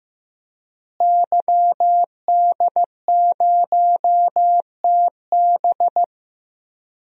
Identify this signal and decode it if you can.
Morse code, heard as a keyed tone, about 15 wpm, 700 Hz. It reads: YD0TB
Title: DXCC-100 Call Signs - 5 Characters - Repeated 2x - 15wpm